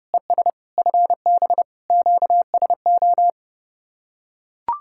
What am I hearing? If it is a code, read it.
Morse code, heard as a keyed tone, about 31 wpm, 700 Hz. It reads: ES FB QSO